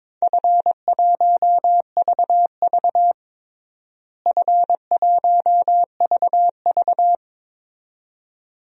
Morse code, heard as a keyed tone, about 22 wpm, 700 Hz. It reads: F1VV F1VV
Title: DXCC-100 Call Signs - 4 Characters - Repeated 2x - 22wpm